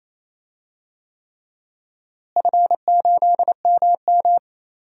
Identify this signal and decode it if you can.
Morse code, heard as a keyed tone, about 28 wpm, 700 Hz. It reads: F8MM
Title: DXCC-100 Call Signs - 4 Characters - Repeated 3x - 28wpm